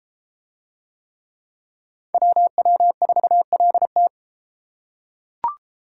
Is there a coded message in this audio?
Yes: Morse code from 2.1 to 4.1 s, heard as a keyed tone, about 33 wpm, 700 Hz. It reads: WW4LT